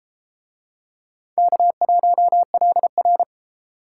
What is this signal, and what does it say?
Morse code, heard as a keyed tone, about 30 wpm, 700 Hz. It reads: K1LR